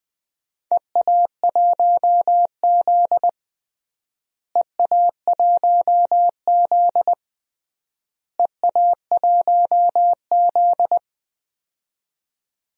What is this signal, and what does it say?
Morse code, heard as a keyed tone, about 20 wpm, 700 Hz. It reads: EA1Z EA1Z EA1Z
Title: DXCC-100 Call Signs - 4 Characters - Repeated 3x - 20wpm